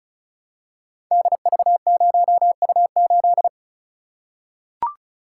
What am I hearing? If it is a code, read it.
Morse code, heard as a keyed tone, about 36 wpm, 700 Hz. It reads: DV0U8
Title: Code Groups - 5 Letter:Numbers - 35wpm